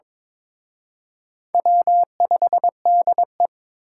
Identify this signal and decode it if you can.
Morse code, heard as a keyed tone, about 27 wpm, 700 Hz. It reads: W5DE
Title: Speed Racing - USA Call Signs 22wpm